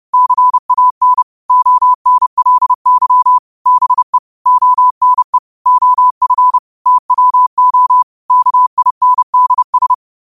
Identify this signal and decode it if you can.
Morse code, heard as a keyed tone, about 30 wpm, 1000 Hz. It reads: CAN ONLY BE ONE OF TWO KINDS
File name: qotdfiles / 30wpm